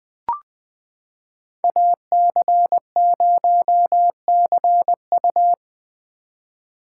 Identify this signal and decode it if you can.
Morse code, heard as a keyed tone, about 20 wpm, 700 Hz. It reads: AC0CU
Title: 2023 POTA Hunter Call Signs - 20wpm